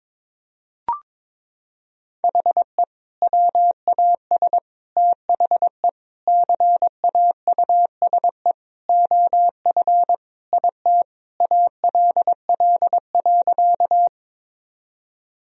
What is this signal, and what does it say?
Morse code, heard as a keyed tone, about 22 wpm, 700 Hz. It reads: HE WAS THE CAUSE OF IT ALL.
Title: Sentences from Top 1000 Words - Review Part 1 - 22wpm